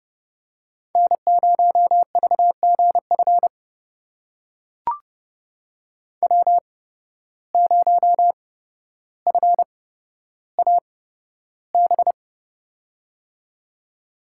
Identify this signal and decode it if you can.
Morse code, heard as a keyed tone, about 31 wpm, spaced as for 13 wpm, 700 Hz. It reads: N0VGF W0FAB
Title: USA Call Signs - 5 Characters - 8x Character Spacing - 30wpm